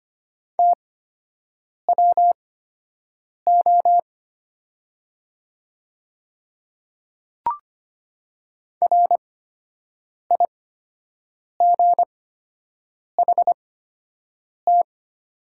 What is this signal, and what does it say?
Morse code, heard as a keyed tone, about 25 wpm, spaced as for 7 wpm, 700 Hz. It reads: TWO RIGHT
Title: 8x Character Spacing - Top 300 Words - 25wpm